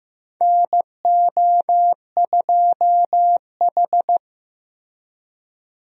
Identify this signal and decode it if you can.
Morse code, heard as a keyed tone, about 15 wpm, 700 Hz. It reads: NO2H
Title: USA Call Signs - 4 Characters - Repeated 2x - 15wpm